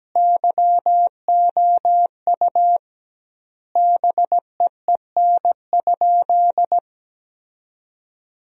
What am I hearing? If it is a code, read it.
Morse code, heard as a keyed tone, about 17 wpm, 700 Hz. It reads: YOU BEEN?
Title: 2x Word-Spacing - Sentences from Top 200 Words - 17wpm